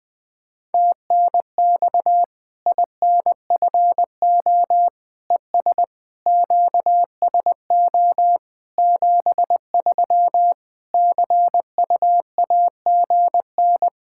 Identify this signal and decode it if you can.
Morse code, heard as a keyed tone, about 21 wpm, 700 Hz. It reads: TNX INFO ES QSO 73 CUAGN
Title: Practice QSO - Exchange 4 - Ending - 20wpm